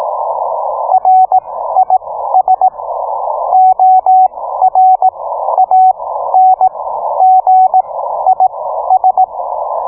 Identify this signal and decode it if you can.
Morse code, heard as a keyed tone, about 18 wpm, spaced as for 12 wpm, 750 Hz. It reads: RIS ORANGIS